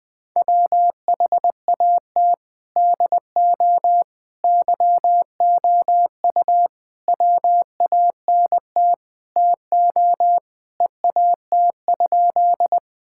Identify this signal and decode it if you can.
Morse code, heard as a keyed tone, about 20 wpm, 700 Hz. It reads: WHAT DO YOU WANT TO EAT?
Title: Sentences with 5-Word Phrases - 20wpm